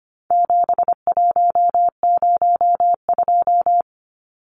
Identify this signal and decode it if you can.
Morse code, heard as a keyed tone, about 25 wpm, 700 Hz. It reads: 7102
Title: Numbers - 4-Digit 25wpm